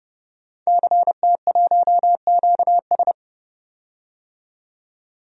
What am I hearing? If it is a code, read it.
Morse code, heard as a keyed tone, about 30 wpm, 700 Hz. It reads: CT1QS
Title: DXCC-100 Call Signs - 5 Characters - 30wpm